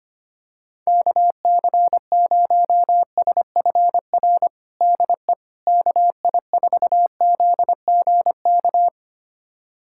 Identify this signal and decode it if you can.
Morse code, heard as a keyed tone, about 25 wpm, 700 Hz. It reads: KC0SFR DE KI4ZGK
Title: USA Callsign DE Callsign - 6 Characters - 25wpm